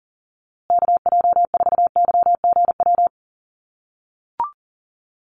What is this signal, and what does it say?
Morse code, heard as a keyed tone, about 40 wpm, 700 Hz. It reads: KJ4YGW
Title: USA Call Signs Repeated 3x 40wpm